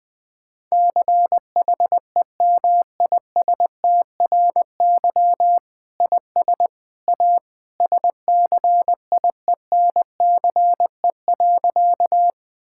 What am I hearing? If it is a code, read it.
Morse code, heard as a keyed tone, about 20 wpm, 700 Hz. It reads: CHEMISTRY IS A SCIENCE.
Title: Sentences from Top 1000 Words - Review Part 6 - 20wpm